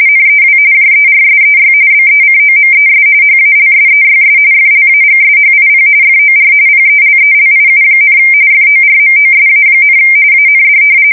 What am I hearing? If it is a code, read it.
RTTY, about 45.45 baud. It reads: NEW JERSEY BECOMES HALSEYS FLAGSHIP FOR PELELIU, FORMOSA, AND
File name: fsk170 / battleshipnj